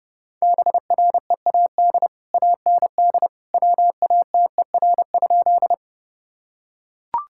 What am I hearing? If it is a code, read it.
Morse code, heard as a keyed tone, about 28 wpm, 700 Hz. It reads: BREAD AND WATER?
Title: Sentences from Top 1600 Words 30wpm